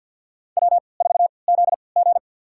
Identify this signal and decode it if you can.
Morse code, heard as a keyed tone, about 51 wpm, spaced as for 28 wpm, 700 Hz. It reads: W4ZG
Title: USA Call Signs - 4 Characters - 3x Character Spacing - 50wpm